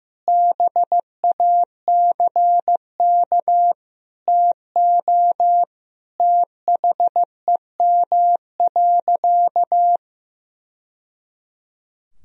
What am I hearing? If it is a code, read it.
Morse code, heard as a keyed tone, about 15 wpm, 700 Hz. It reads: BACK TO THEM.
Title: Sentences from Top 1000 Words - Review Part 3 - 15wpm